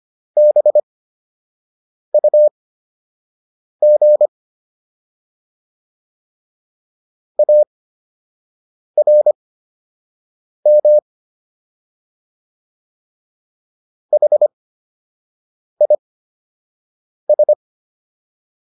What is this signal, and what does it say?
Morse code, heard as a keyed tone, about 25 wpm, spaced as for 6 wpm, 600 Hz. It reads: BUG ARM HIS